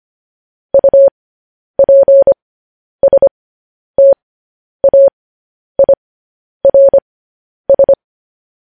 Morse code, heard as a keyed tone, about 25 wpm, spaced as for 10 wpm, 550 Hz. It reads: UPSTAIRS
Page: news headlines in morse code at 10 wpm